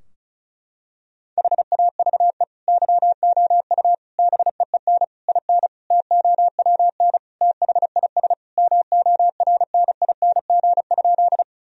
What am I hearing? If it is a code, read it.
Morse code, heard as a keyed tone, about 36 wpm, 700 Hz. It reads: HAVE YOU BEEN IN TOWN THIS MORNING?
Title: Sentences from Top 1000 Words - Review Part 2 - 35wpm